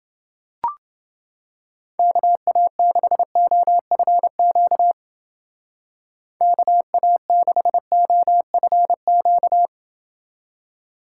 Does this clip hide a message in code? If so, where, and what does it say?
Yes: Morse code from 2.0 to 9.7 s, heard as a keyed tone, about 31 wpm, 700 Hz. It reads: KA6OFQ KA6OFQ